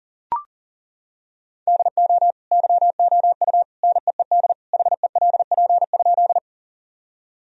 Morse code, heard as a keyed tone, about 40 wpm, 700 Hz. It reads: DO YOU NEED HELP?
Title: Sentences from Top 1000 Words - Review Part 1 - 40wpm